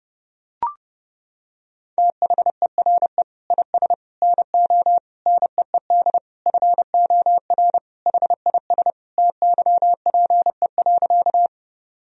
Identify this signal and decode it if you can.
Morse code, heard as a keyed tone, about 30 wpm, 700 Hz. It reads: THERE IS NO NEED FOR HIS TYPE.
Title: Sentences from Top 1000 Words - Review Part 3 - 30wpm